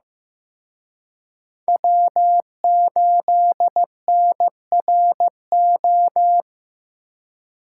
Morse code, heard as a keyed tone, about 19 wpm, 700 Hz. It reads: W8NRO